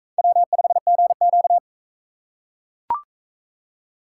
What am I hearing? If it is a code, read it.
Morse code, heard as a keyed tone, about 42 wpm, 700 Hz. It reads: W5GQ